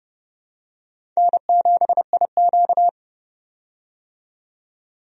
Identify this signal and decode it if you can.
Morse code, heard as a keyed tone, about 30 wpm, 700 Hz. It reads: N7IQ
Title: USA Call Signs - 4 Characters - 30wpm